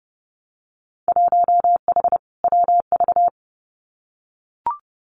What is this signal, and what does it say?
Morse code, heard as a keyed tone, about 30 wpm, 700 Hz. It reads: 1H WV
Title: Winter Field Day Exchanges 30wpm